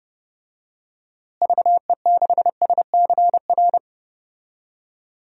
Morse code, heard as a keyed tone, about 32 wpm, 700 Hz. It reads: VE6SCR
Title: DXCC-100 Call Signs - 6 Characters - Repeated 3x - 30wpm